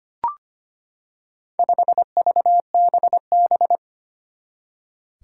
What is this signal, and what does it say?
Morse code, heard as a keyed tone, about 25 wpm, 700 Hz. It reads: 5VBB
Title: Four Challenging Letter-Number Combinations 25wpm